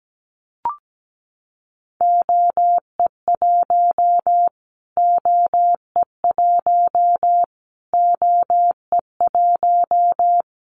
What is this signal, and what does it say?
Morse code, heard as a keyed tone, about 17 wpm, 700 Hz. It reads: OE1 OE1 OE1
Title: Callsign Prefixes for Top 25 DXCC - Repeated 3x - 17wpm